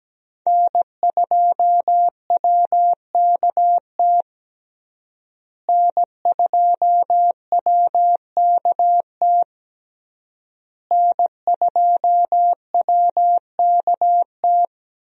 Morse code, heard as a keyed tone, about 17 wpm, 700 Hz. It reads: N2WKT N2WKT N2WKT
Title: USA Call Signs - 5 Characters - Repeated 3x - 17wpm